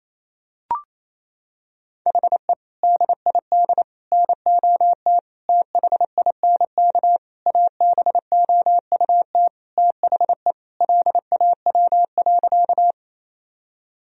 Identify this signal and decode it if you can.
Morse code, heard as a keyed tone, about 28 wpm, 700 Hz. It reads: HE DID NOT THINK ABOUT THE LAW.
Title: Sentences from Top 1000 Words - Review Part 3 - 28wpm